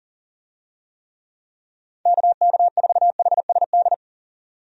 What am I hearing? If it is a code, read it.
Morse code, heard as a keyed tone, about 40 wpm, 700 Hz. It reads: KK4HSD
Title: USA Call Signs - 6 Characters - 40wpm